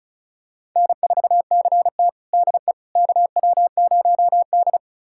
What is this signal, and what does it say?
Morse code, heard as a keyed tone, about 36 wpm, 700 Hz. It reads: N4CT DE KW0D
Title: USA Callsign DE Callsign - 4 Characters - 35wpm